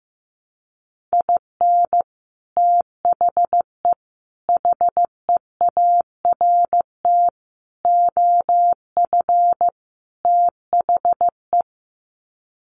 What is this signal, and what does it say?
Morse code, heard as a keyed tone, about 15 wpm, 700 Hz. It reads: IN THE HEART OF THE
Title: Sets of 5 Words 15wpm